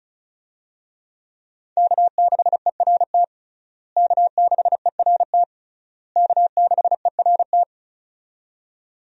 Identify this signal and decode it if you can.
Morse code, heard as a keyed tone, about 34 wpm, 700 Hz. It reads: K6ERT K6ERT K6ERT
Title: USA Call Signs - 5 Characters - Repeated 3x - 35wpm